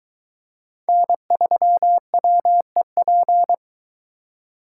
Morse code, heard as a keyed tone, about 22 wpm, 700 Hz. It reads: N3WEP